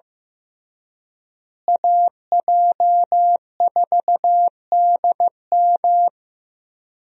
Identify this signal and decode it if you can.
Morse code, heard as a keyed tone, about 15 wpm, 700 Hz. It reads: AJ4DM
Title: USA Call Signs - 5 Characters - Repeated 2x - 15wpm